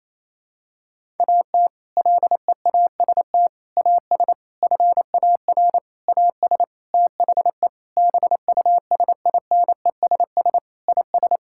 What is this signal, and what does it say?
Morse code, heard as a keyed tone, about 28 wpm, 700 Hz. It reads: AT LEAST AS FAR AS THE BUSINESS IS
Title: Sentences from Top 1000 Words - Review Part 6 - 28wpm